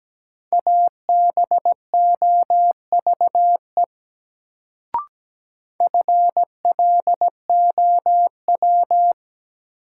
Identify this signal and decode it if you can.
Morse code, heard as a keyed tone, about 17 wpm, 700 Hz. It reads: ABOVE FLOW
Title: Sentences from Top 1200 Words - New Word Drill Alt - 17wpm